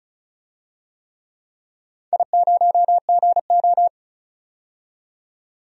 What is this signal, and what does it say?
Morse code, heard as a keyed tone, about 36 wpm, 700 Hz. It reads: I0GO
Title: DXCC-100 Call Signs - 4 Characters - 35wpm